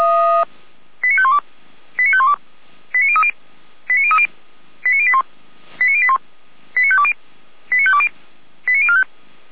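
Five-tone selective call (ZVEI), heard as quick runs of five tones, tones of about 70 ms. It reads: 89512 89512 89030 89030 89081 89081 89520 89520 89045
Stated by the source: ZVEI five-tone selective call 89115 received on SDR.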